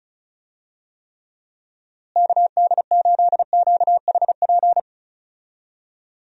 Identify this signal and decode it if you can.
Morse code, heard as a keyed tone, about 36 wpm, 700 Hz. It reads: KD8QHP
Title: USA Call Signs - 6 Characters - Repeated 2x - 35wpm